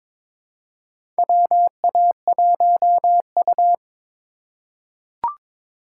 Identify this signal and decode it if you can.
Morse code, heard as a keyed tone, about 23 wpm, 700 Hz. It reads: WA1U
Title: USA Call Signs - 4 Characters - 8x Character Spacing - 22wpm